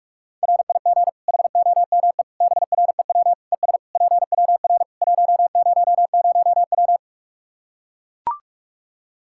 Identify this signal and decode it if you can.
Morse code, heard as a keyed tone, about 41 wpm, 700 Hz. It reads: RIG HOME BREW ES PWR 100W
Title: Practice QSO - Exchange 2 - Info - 45wpm